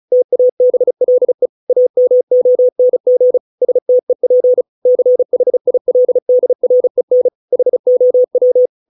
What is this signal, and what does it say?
Morse code, heard as a keyed tone, about 34 wpm, 500 Hz. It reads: TABLE AMONG STEP CHILDREN HOW